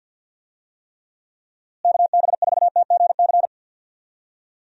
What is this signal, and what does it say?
Morse code, heard as a keyed tone, about 52 wpm, 700 Hz. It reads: KB4TGC
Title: USA Call Signs - 6 Characters - Repeated 2x - 50wpm